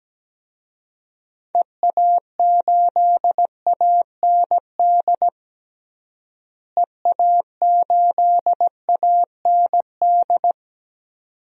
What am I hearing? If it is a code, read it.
Morse code, heard as a keyed tone, about 17 wpm, 700 Hz. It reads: EA8AND EA8AND
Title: DXCC-100 Call Signs - 6 Characters - Repeated 2x - 17wpm